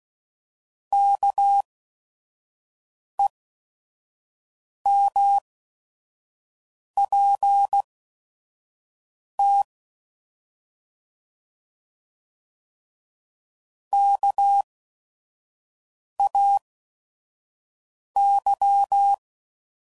Morse code, heard as a keyed tone, about 16 wpm, spaced as for 5 wpm, 800 Hz. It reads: KEMPT KAY
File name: Prefixes / By First Letter / Ks